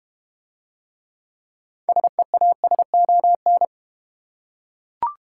Morse code, heard as a keyed tone, about 31 wpm, 700 Hz. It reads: SEASON
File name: Easy Words - Speed-Racing with Hints - 25wpm